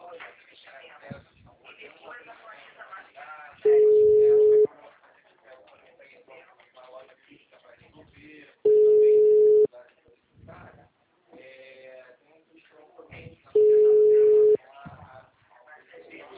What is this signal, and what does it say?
Ringback tone (Europe, 425 Hz), with speech.
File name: out- -2035- - - .13793